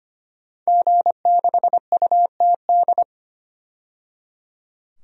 Morse code, heard as a keyed tone, about 25 wpm, 700 Hz. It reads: G6UTD
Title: DXCC-100 Call Signs - 5 Characters - 25wpm